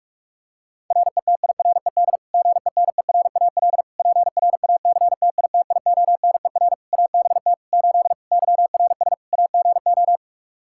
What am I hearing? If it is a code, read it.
Morse code, heard as a keyed tone, about 46 wpm, 700 Hz. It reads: RETIRED GENERAL PRACTITIONER ABT 8 YRS AGO
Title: Practice QSO - Exchange 3 - More Info - 45wpm